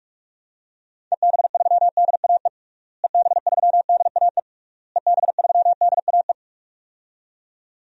Morse code, heard as a keyed tone, about 41 wpm, 700 Hz. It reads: EB3DAE EB3DAE EB3DAE
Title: DXCC-100 Call Signs - 6 Characters - Repeated 3x - 45wpm